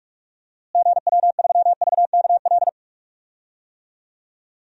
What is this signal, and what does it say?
Morse code, heard as a keyed tone, about 46 wpm, 700 Hz. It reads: GW3VKL
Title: DXCC-100 Call Signs - 6 Characters - Repeated 3x - 45wpm